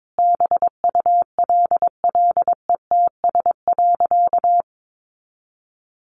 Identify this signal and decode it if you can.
Morse code, heard as a keyed tone, about 22 wpm, 700 Hz. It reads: BULLETS.
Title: A Sound Of Thunder 22wpm